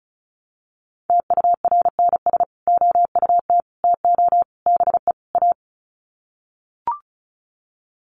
Morse code, heard as a keyed tone, about 34 wpm, 700 Hz. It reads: TURNS OUT TO BE A
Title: Sets of 5 Words Encore 35wpm